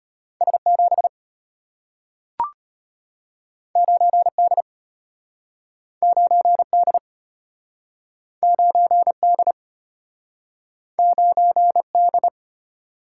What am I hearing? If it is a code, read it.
Morse code, heard as a keyed tone, about 35 wpm, 690 Hz. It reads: S7 9D 9D 9D 9D